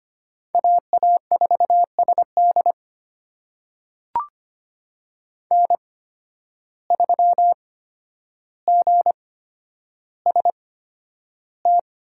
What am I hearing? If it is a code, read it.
Morse code, heard as a keyed tone, about 25 wpm, spaced as for 11 wpm, 700 Hz. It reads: AA4SD N3GST
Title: USA Call Signs - 5 Characters - 8x Character Spacing - 25wpm